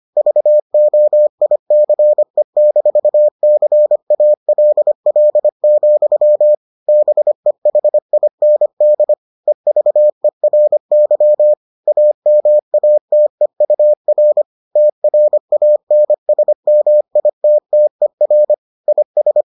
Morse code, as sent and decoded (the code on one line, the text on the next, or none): ...- --- .. -.-. -....- -.-. .- .-.. .-.. --..-- / -... .... .. -. -.. / . ...- . .-. -.-- / .- -- .- - . ..- .-. / - .-. .- -. ... -- .. - - . .-. / .. ...
VOICE-CALL, BEHIND EVERY AMATEUR TRANSMITTER IS